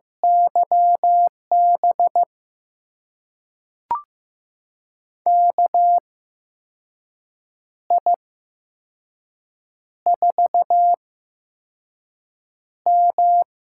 Morse code, heard as a keyed tone, about 15 wpm, spaced as for 5 wpm, 700 Hz. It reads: YB KI4M
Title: USA Call Signs - 6 Characters - 8x Character Spacing - 15wpm